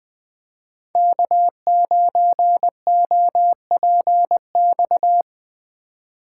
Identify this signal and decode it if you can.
Morse code, heard as a keyed tone, about 20 wpm, 700 Hz. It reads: K9OPX
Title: Speed Racing - USA 1x3 Call Signs 17wpm